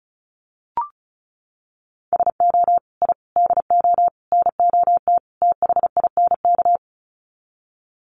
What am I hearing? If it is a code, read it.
Morse code, heard as a keyed tone, about 34 wpm, 700 Hz. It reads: SO I DO NOT THINK
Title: Sets of 5 Words 35wpm